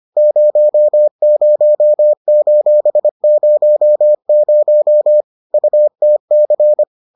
Morse code, heard as a keyed tone, about 25 wpm, 600 Hz. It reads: 00:00 UTC